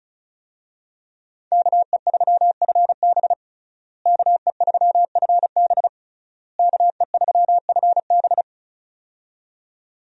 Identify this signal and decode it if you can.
Morse code, heard as a keyed tone, about 34 wpm, 700 Hz. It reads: KE3FB KE3FB KE3FB
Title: USA Call Signs - 5 Characters - Repeated 3x - 35wpm